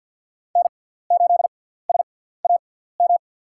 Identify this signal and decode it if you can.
Morse code, heard as a keyed tone, about 50 wpm, spaced as for 17 wpm, 700 Hz. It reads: N8SAM
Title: USA Call Signs - 5 Characters - 6x Character Spacing - 50wpm